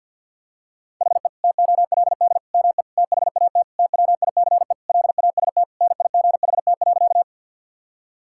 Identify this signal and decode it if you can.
Morse code, heard as a keyed tone, about 51 wpm, 700 Hz. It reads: HE TOLD ME THAT TWICE LAST NIGHT.
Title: Sentences from Top 1600 Words 50wpm